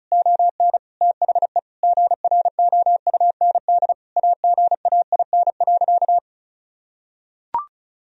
Morse code, heard as a keyed tone, about 36 wpm, 700 Hz. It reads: ON THE GROUND AGAIN.
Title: Sentences from Top 1000 Words - Review Part 2 - 35wpm